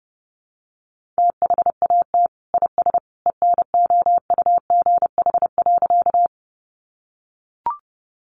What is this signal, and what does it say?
Morse code, heard as a keyed tone, about 28 wpm, 700 Hz. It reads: THAT IS ENOUGH.
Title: Sentences from Top 400 Words 30wpm